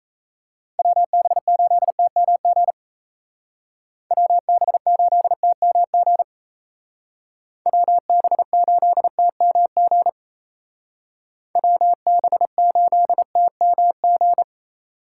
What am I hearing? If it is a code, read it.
Morse code, heard as a keyed tone, about 35 wpm, 700 Hz. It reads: WB8TMG WB8TMG WB8TMG WB8TMG